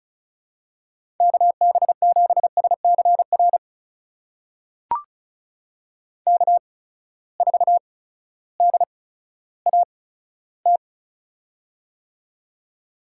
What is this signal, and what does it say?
Morse code, heard as a keyed tone, about 36 wpm, spaced as for 16 wpm, 700 Hz. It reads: KB7SCR K4DAT
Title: USA Call Signs - 8x Character Spacing - 35wpm